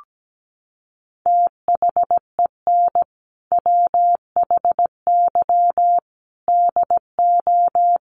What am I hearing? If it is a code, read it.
Morse code, heard as a keyed tone, about 17 wpm, 700 Hz. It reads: THEN WHY DO
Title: Sentences from Top 1200 Words 17wpm